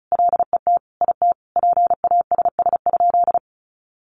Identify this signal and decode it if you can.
Morse code, heard as a keyed tone, about 34 wpm, 700 Hz. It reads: LET IT PASS?
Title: Sentences from Top 400 Words 35wpm